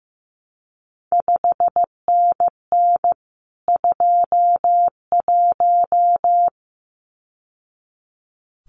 Morse code, heard as a keyed tone, about 15 wpm, 700 Hz. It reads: 5NN 21
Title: CQ WW DX Contest Exchange Encore - 15wpm